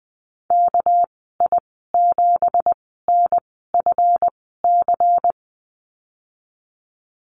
Morse code, heard as a keyed tone, about 20 wpm, spaced as for 15 wpm, 700 Hz. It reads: KI7NFC
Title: 219 Lesson 35g - Number 0 - Call Signs 20wpm